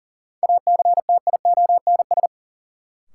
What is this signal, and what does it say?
Morse code, heard as a keyed tone, about 39 wpm, 700 Hz. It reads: ACTIONS
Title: Sentences from Top 1000 Words - Review Part 6 - New Word Drill - 40wpm